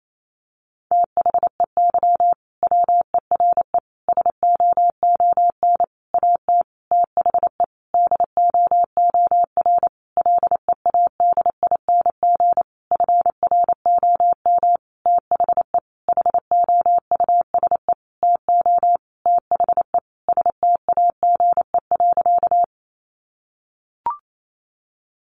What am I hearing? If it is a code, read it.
Morse code, heard as a keyed tone, about 28 wpm, 700 Hz. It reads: THEY WERE SOON AT THE DOOR LEADING FROM THE HOUSE TO THE STAGE.
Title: Sentences from Top 1200 Words 28wpm